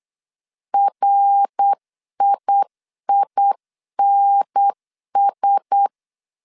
Morse code, heard as a keyed tone, about 8 wpm, 800 Hz. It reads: RIINS
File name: CW / DET / 1 6 1500C S C 8wpm 5219